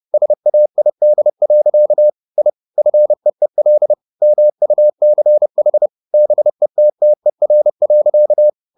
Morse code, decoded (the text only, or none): SAID. I FEEL MUCH BETTER.